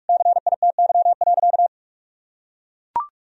Morse code, heard as a keyed tone, about 41 wpm, 700 Hz. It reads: CITY.
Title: Sentences from Top 1000 Words - Review Part 3 - 45wpm